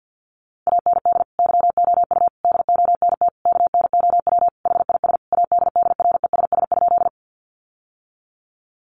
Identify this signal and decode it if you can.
Morse code, heard as a keyed tone, about 50 wpm, 700 Hz. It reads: AND YOU DONT KNOW HIS ADDRESS?
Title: Sentences from Top 1000 Words 50wpm